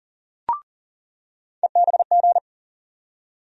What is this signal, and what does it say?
Morse code, heard as a keyed tone, about 40 wpm, 700 Hz. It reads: EBG
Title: USA Call Sign Postfixes - 3 Letters 40wpm